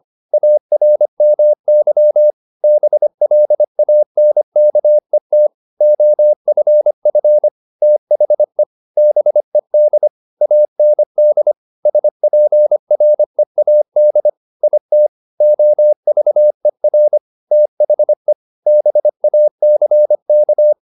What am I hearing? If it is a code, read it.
Morse code, heard as a keyed tone, about 25 wpm, 600 Hz. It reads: ARMY BLANKET OFF THE BED AND SPREAD IT OVER THE BACK
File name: the old man and the sea part 1 25wpm